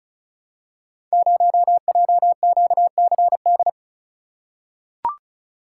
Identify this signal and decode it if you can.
Morse code, heard as a keyed tone, about 36 wpm, 700 Hz. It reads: 0JQCD
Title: Code Groups - 5 Letter:Numbers - 35wpm